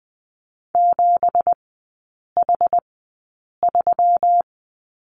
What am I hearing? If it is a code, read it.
Morse code, heard as a keyed tone, about 20 wpm, spaced as for 8 wpm, 700 Hz. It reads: 7H3
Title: 198 Lesson 32e - Letter k - Three Characters 20wpm